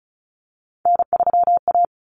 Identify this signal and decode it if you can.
Morse code, heard as a keyed tone, about 34 wpm, 700 Hz. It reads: N3A
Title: Common Cut Numbers - 3-Digit 35wpm